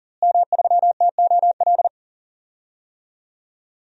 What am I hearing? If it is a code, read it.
Morse code, heard as a keyed tone, about 40 wpm, 700 Hz. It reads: M3TOL
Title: DXCC-100 Call Signs - 5 Characters - Repeated 3x - 40wpm